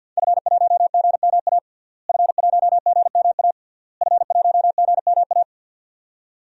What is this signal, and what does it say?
Morse code, heard as a keyed tone, about 52 wpm, 700 Hz. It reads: F1GMA F1GMA F1GMA
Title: DXCC-100 Call Signs - 5 Characters - Repeated 3x - 50wpm